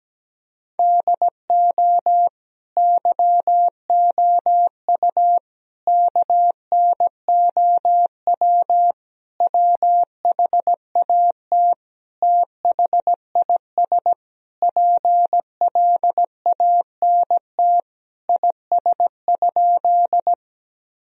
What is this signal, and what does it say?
Morse code, heard as a keyed tone, about 17 wpm, 700 Hz. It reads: DO YOU KNOW WHAT THIS PLANT IS?
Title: Sentences from Top 1000 Words - Review Part 1 - 17wpm